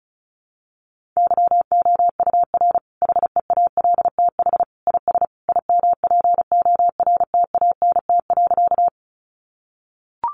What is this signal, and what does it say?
Morse code, heard as a keyed tone, about 34 wpm, 700 Hz. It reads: YOUR HEALTH IS IMPORTANT.
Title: Sentences from Top 600 Words 35wpm